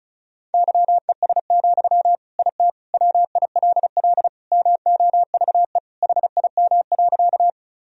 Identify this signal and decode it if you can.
Morse code, heard as a keyed tone, about 34 wpm, 700 Hz. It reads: YES, IT WILL MOVE HIM.
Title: Sentences from Top 1000 Words - Review Part 1 - 35wpm